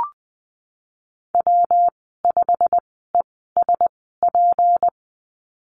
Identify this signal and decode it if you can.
Morse code, heard as a keyed tone, about 20 wpm, spaced as for 15 wpm, 700 Hz. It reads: W5ESP
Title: 186 Lesson 30g - Letter b - Call Signs 20wpm